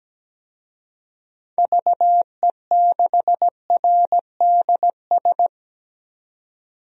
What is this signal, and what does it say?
Morse code, heard as a keyed tone, about 17 wpm, 700 Hz. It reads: VE6RDS
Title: DXCC-100 Call Signs - 6 Characters - 17wpm